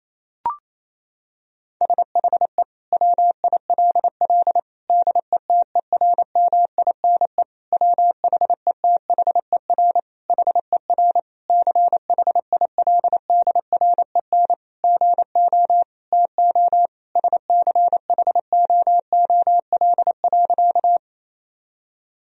Morse code, heard as a keyed tone, about 28 wpm, 700 Hz. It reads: SHE WILL DETERMINE WHETHER HER CHILDREN GO TO SCHOOL.
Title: Sentences from Top 1000 Words - Review Part 4 - 28wpm